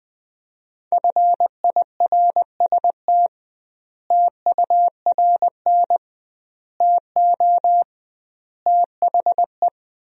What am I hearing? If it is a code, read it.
Morse code, heard as a keyed tone, about 20 wpm, 700 Hz. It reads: FIRST TURN TO THE
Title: 2x Word-Spacing - Sentences from Top 300 Words - 20wpm